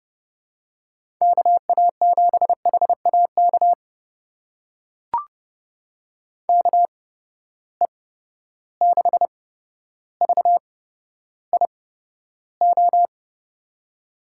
Morse code, heard as a keyed tone, about 30 wpm, spaced as for 13 wpm, 700 Hz. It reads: KA7HAK KE6VIO
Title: USA Call Signs - 6 Characters - 8x Character Spacing - 30wpm